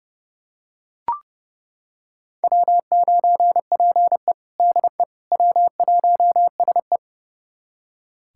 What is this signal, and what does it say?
Morse code, heard as a keyed tone, about 29 wpm, 700 Hz. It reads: W9PE DE W1SE
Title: USA Callsign DE Callsign - 4 Characters - 30wpm